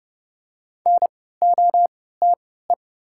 Morse code, heard as a keyed tone, about 30 wpm, spaced as for 17 wpm, 690 Hz. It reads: NOTE
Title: 3x Character Spacing - Top 500 Words - 30wpm